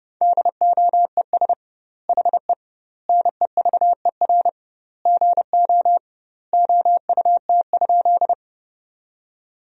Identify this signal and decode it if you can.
Morse code, heard as a keyed tone, about 30 wpm, 700 Hz. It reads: DOES HE NEVER GO OUT?
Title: 2x Word-Spacing - Sentences from Top 300 Words - 30wpm